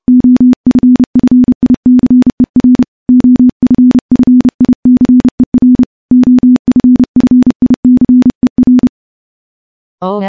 Morse code, heard as a keyed tone, about 30 wpm, 250 Hz. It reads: OFFICER OFFICER OFFICER
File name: morse / morse-training-30WPM